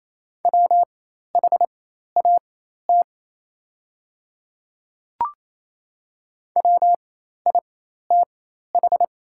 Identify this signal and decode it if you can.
Morse code, heard as a keyed tone, about 27 wpm, spaced as for 13 wpm, 690 Hz. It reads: WHAT WITH
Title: 4x Character Spacing - Top 100 Words - 28wpm